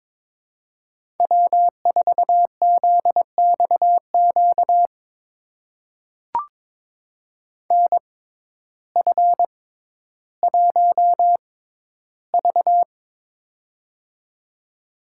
Morse code, heard as a keyed tone, about 22 wpm, spaced as for 12 wpm, 700 Hz. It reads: W4ZXQ NF1V
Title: USA Call Signs - 6x Character Spacing - 22wpm